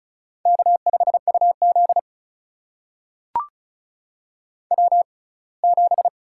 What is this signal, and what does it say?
Morse code, heard as a keyed tone, about 36 wpm, spaced as for 24 wpm, 700 Hz. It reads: K5UZ W7
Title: USA Call Signs - 4 Characters - 6x Character Spacing - 35wpm